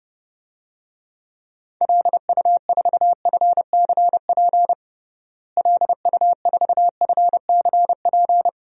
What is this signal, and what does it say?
Morse code, heard as a keyed tone, about 30 wpm, 700 Hz. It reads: LU4FCP LU4FCP
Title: DXCC-100 Call Signs - 6 Characters - Repeated 2x - 30wpm